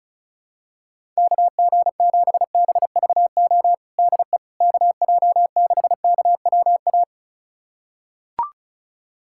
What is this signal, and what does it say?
Morse code, heard as a keyed tone, about 36 wpm, 700 Hz. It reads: KG7BVO DE KJ6KWA
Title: USA Callsign DE Callsign - 6 Characters - 35wpm